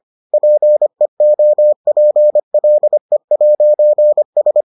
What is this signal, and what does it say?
Morse code, heard as a keyed tone, about 25 wpm, 600 Hz. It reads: PEOPLE'S